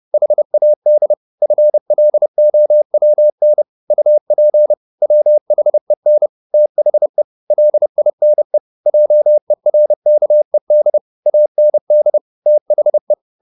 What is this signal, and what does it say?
Morse code, heard as a keyed tone, about 30 wpm, 600 Hz. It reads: HAD FLOWN UP WHEN THE LINE JERKED AND THE
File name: the old man and the sea / the old man and the sea 1 30wpm